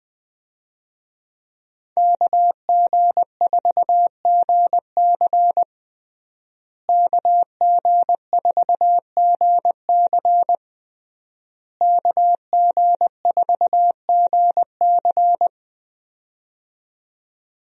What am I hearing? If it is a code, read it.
Morse code, heard as a keyed tone, about 20 wpm, 700 Hz. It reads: KG4GC KG4GC KG4GC
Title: USA Call Signs - 5 Characters - Repeated 3x - 20wpm